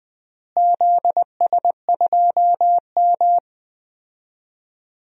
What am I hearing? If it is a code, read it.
Morse code, heard as a keyed tone, about 20 wpm, 700 Hz. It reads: ZS2M
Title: DXCC-100 Call Signs - 4 Characters - Repeated 3x - 20wpm